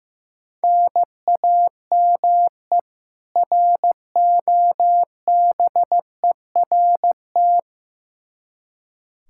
Morse code, heard as a keyed tone, about 15 wpm, 700 Hz. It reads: NAME ROBERT
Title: Practice QSO - Exchange 1 - Beginning - 15wpm